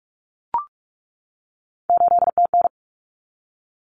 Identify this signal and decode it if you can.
Morse code, heard as a keyed tone, about 44 wpm, 700 Hz. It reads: 8TN
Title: Common Cut Numbers - 3-Digit 45wpm